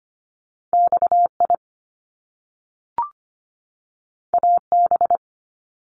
Morse code, heard as a keyed tone, about 25 wpm, 700 Hz. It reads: XI AB
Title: Top Two Letter Combinations in Words 25wpm